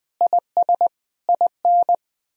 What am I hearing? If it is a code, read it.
Morse code, heard as a keyed tone, about 20 wpm, 700 Hz. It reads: IS IN
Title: Sentences from Top 1000 Words - Review Part 4 - 20wpm